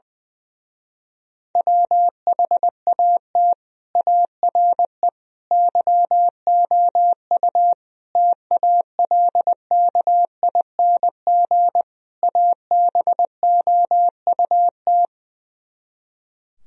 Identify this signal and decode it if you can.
Morse code, heard as a keyed tone, about 20 wpm, 700 Hz. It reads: WHAT ARE YOU TALKING ABOUT
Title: Speed-Racing - Sets of 5 Words 20wpm